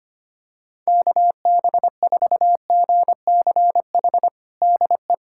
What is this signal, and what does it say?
Morse code, heard as a keyed tone, about 25 wpm, 700 Hz. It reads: KB4GCH DE
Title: USA Callsign DE Callsign 25wpm